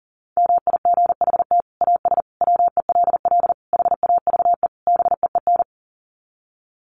Morse code, heard as a keyed tone, about 40 wpm, 700 Hz. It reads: MIGHT AS WELL HAVE BEEN
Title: Sets of 5 Words Encore2 40wpm